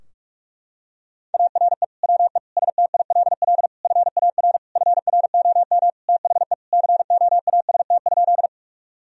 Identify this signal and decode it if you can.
Morse code, heard as a keyed tone, about 41 wpm, 700 Hz. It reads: ARE WE STILL FAR FROM THE COAST?
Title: Sentences from Top 1000 Words - Review Part 4 - 45wpm